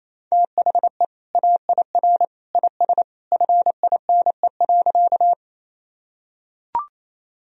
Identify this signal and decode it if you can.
Morse code, heard as a keyed tone, about 28 wpm, 700 Hz. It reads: THE AIR IS FINE.
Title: Sentences from Top 1000 Words - Review Part 1 - 28wpm